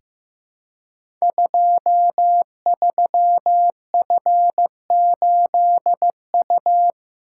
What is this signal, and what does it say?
Morse code, heard as a keyed tone, about 15 wpm, 700 Hz. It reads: 23F8U
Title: Code Groups - 5 Letter:Numbers - 15wpm